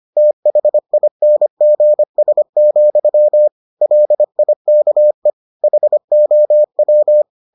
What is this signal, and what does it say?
Morse code, heard as a keyed tone, about 25 wpm, 600 Hz. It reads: THINGS, LIKE HOW